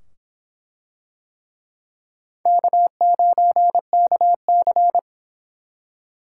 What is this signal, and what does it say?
Morse code, heard as a keyed tone, about 24 wpm, 700 Hz. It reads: K9KC